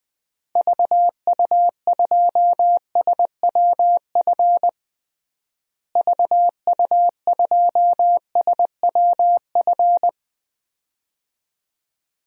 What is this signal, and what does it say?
Morse code, heard as a keyed tone, about 21 wpm, 700 Hz. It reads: VU2SWF VU2SWF
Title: DXCC-100 Call Signs - 6 Characters - Repeated 2x - 20wpm